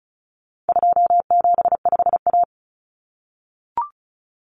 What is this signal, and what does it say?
Morse code, heard as a keyed tone, about 34 wpm, 700 Hz. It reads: 275A
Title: Common Cut Numbers - 4-Digit 35wpm